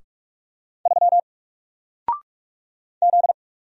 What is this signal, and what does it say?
Morse code, heard as a keyed tone, about 44 wpm, 700 Hz. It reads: 3 Z
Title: Single Letter-Number 45wpm